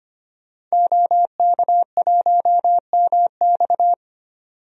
Morse code, heard as a keyed tone, about 25 wpm, 700 Hz. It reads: OK1MX
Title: DXCC-100 Call Signs - 5 Characters - 25wpm